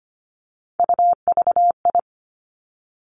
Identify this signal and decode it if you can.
Morse code, heard as a keyed tone, about 25 wpm, 700 Hz. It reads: UVI
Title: Three Challenging Letter-Number Combinations 25wpm